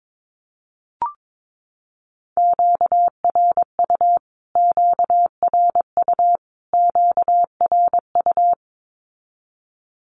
Morse code, heard as a keyed tone, about 22 wpm, 700 Hz. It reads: QRU QRU QRU
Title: Q-Codes Repeated 3x 22wpm